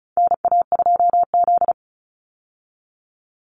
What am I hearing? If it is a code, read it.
Morse code, heard as a keyed tone, about 34 wpm, 700 Hz. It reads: NA2Z
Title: USA Call Signs 35wpm